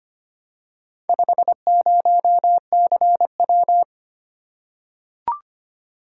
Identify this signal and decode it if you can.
Morse code, heard as a keyed tone, about 25 wpm, 700 Hz. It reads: 50CW
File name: practice / 25wpm / Four Letter-Number Combinations - 25wpm